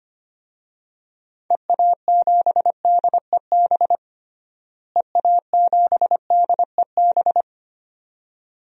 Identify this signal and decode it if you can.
Morse code, heard as a keyed tone, about 25 wpm, 700 Hz. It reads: EA7DEB EA7DEB
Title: DXCC-100 Call Signs - 6 Characters - Repeated 2x - 25wpm